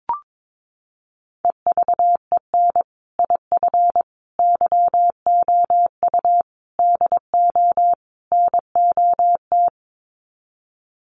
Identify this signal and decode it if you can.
Morse code, heard as a keyed tone, about 22 wpm, 700 Hz. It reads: EVEN IF YOU DO NOT
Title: Sets of 5 Words Encore 22wpm